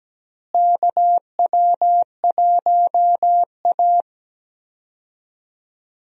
Morse code, heard as a keyed tone, about 17 wpm, 700 Hz. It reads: KW1A
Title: USA Call Signs - 4 Characters - Repeated 2x - 17wpm